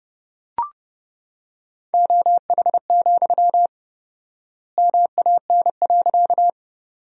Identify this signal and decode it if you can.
Morse code, heard as a keyed tone, about 30 wpm, 700 Hz. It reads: OH, MAN.
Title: 4x Word-Spacing - Sentences from Top 500 Words - 30wpm